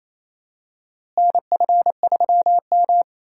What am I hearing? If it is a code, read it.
Morse code, heard as a keyed tone, about 27 wpm, 700 Hz. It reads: NF3M
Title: USA Call Signs - 4 Characters - Repeated 2x - 28wpm